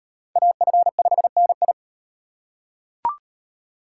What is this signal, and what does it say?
Morse code, heard as a keyed tone, about 38 wpm, 700 Hz. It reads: AF5NI